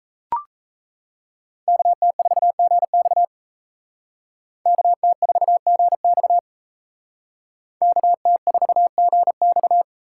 Morse code, heard as a keyed tone, about 36 wpm, 700 Hz. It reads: KT4GX KT4GX KT4GX